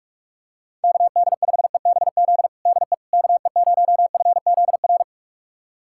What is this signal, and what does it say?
Morse code, heard as a keyed tone, about 46 wpm, 700 Hz. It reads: KD5EBZ DE KE0FZR
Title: USA Callsign DE Callsign - 6 Characters - 45wpm